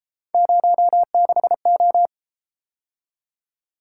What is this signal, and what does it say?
Morse code, heard as a keyed tone, about 33 wpm, 700 Hz. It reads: 06O